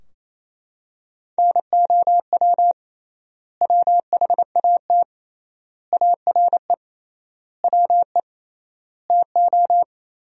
Morse code, heard as a keyed tone, about 28 wpm, 700 Hz. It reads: NOW WHAT ARE WE TO
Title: 3x Word-Spacing - Sentences from Top 100 Words - 28wpm